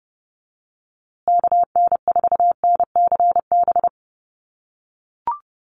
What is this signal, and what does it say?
Morse code, heard as a keyed tone, about 30 wpm, 700 Hz. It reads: KN4NCB
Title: USA Call Signs Repeated 3x 30wpm